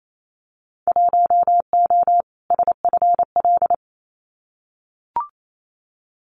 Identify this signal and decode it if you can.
Morse code, heard as a keyed tone, about 28 wpm, 700 Hz. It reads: 1O SFL
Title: Winter Field Day Exchanges Encore 28wpm